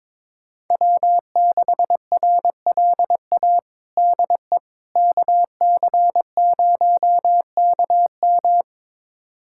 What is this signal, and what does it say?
Morse code, heard as a keyed tone, about 22 wpm, 700 Hz. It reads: W6RLA DE KC0KM
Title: USA Callsign DE Callsign - 5 Characters - 22wpm